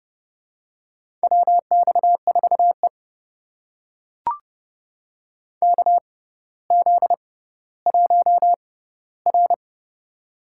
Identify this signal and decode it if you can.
Morse code, heard as a keyed tone, about 30 wpm, spaced as for 15 wpm, 700 Hz. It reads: WX4E KZ1R
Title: USA Call Signs - 4 Characters - 6x Character Spacing - 30wpm